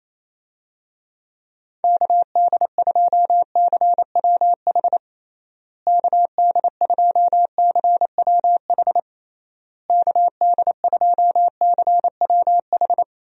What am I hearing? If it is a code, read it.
Morse code, heard as a keyed tone, about 28 wpm, 700 Hz. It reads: KD2CWH KD2CWH KD2CWH
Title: USA Call Signs - 6 Characters - Repeated 3x - 28wpm